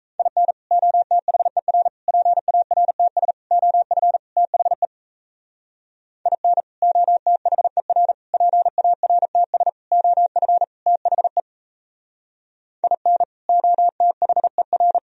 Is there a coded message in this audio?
Yes: Morse code from 0.2 to 15.0 s, heard as a keyed tone, about 36 wpm, 700 Hz. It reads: IN OTHER PARTS OF THE IN OTHER PARTS OF THE IN OTHER